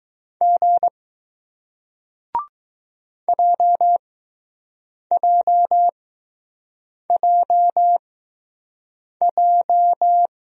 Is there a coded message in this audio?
Yes: Morse code from 0.4 to 10.3 s, heard as a keyed tone, about 20 wpm, spaced as for 6 wpm, 700 Hz. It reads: G JJJJ